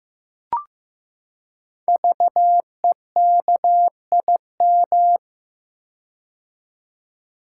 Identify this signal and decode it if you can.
Morse code, heard as a keyed tone, about 15 wpm, 700 Hz. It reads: VEKIM
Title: Code Groups - 5 Letters - 15wpm